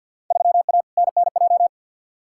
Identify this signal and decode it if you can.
Morse code, heard as a keyed tone, about 53 wpm, 700 Hz. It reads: 3A NNJ
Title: Field Day Exchanges - 50wpm